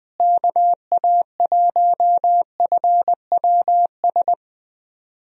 Morse code, heard as a keyed tone, about 20 wpm, 700 Hz. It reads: KA1FWS
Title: USA Call Signs - 6 Characters - Repeated 2x - 20wpm